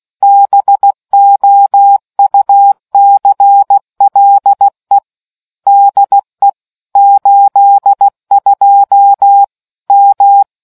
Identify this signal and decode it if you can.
Morse code, heard as a keyed tone, about 16 wpm, 800 Hz. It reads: BOUCLE DE 82 M